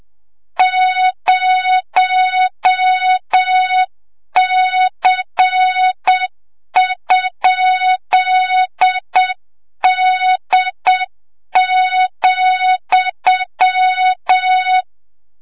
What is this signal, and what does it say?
Morse code, heard as a keyed tone, about 7 wpm, 3700 Hz, with noise. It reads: 0C?D,
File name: Morze-HA2MN / hang / 25 GYAKORLAT